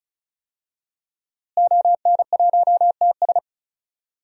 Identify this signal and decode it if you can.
Morse code, heard as a keyed tone, about 36 wpm, 700 Hz. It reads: ON1TS
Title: DXCC-100 Call Signs - 5 Characters - Repeated 2x - 35wpm